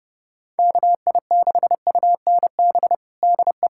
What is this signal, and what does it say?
Morse code, heard as a keyed tone, about 30 wpm, 700 Hz. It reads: KI6UNB DE
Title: USA Callsign DE Callsign - 6 Characters - 30wpm